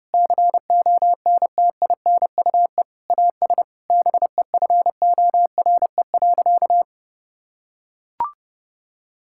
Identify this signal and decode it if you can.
Morse code, heard as a keyed tone, about 32 wpm, 700 Hz. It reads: CONTINUE AS BEFORE.
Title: Sentences from Top 1000 Words - Review Part 6 - 30wpm